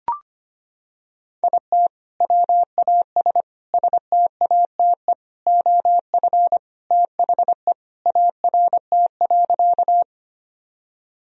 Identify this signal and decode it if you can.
Morse code, heard as a keyed tone, about 25 wpm, 700 Hz. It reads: IT WAS STATE OF THE ART.
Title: Sentences from Top 1000 Words - Review Part 3 - 25wpm